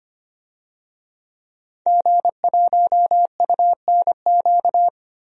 Morse code, heard as a keyed tone, about 25 wpm, 700 Hz. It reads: G1UNQ
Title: DXCC-100 Call Signs - 5 Characters - Repeated 3x - 25wpm